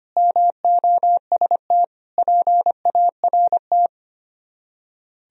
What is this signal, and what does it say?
Morse code, heard as a keyed tone, about 31 wpm, 700 Hz. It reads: MOST PART
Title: Speed-Racing - Sets of 4 Words 25wpm